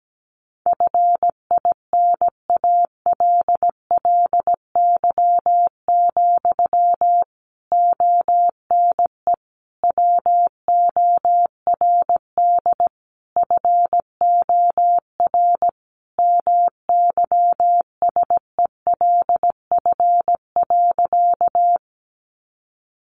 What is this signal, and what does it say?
Morse code, heard as a keyed tone, about 17 wpm, 700 Hz. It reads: FINALLY, ONE WORD FOR MYSELF.
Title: Sentences from Top 1300 Words 17wpm